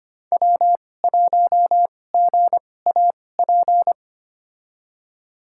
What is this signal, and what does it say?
Morse code, heard as a keyed tone, about 25 wpm, spaced as for 18 wpm, 700 Hz. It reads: W1GAP
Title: USA Call Signs - 5 Characters - 2x Character Spacing - 25wpm